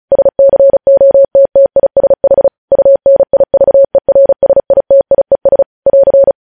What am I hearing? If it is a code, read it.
Morse code, heard as a keyed tone, about 36 wpm, 550 Hz. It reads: SCOTTISH UNIVERSITIES <AR>
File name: podcast / News headlines-1x-35x35